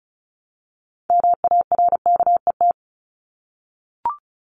Morse code, heard as a keyed tone, about 34 wpm, 700 Hz. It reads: MARKET
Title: Top 501-750 Words 35wpm